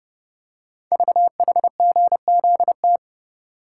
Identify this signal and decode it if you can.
Morse code, heard as a keyed tone, about 28 wpm, 690 Hz. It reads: VHGZT
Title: Code Groups - 5 Letters - 30wpm